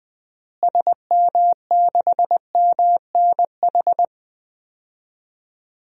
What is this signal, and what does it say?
Morse code, heard as a keyed tone, about 20 wpm, 700 Hz. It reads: SM6MNH
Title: DXCC-100 Call Signs - 6 Characters - Repeated 2x - 20wpm